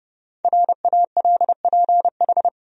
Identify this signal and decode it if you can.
Morse code, heard as a keyed tone, about 30 wpm, 700 Hz. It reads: RALPH
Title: USA Common Names - 30wpm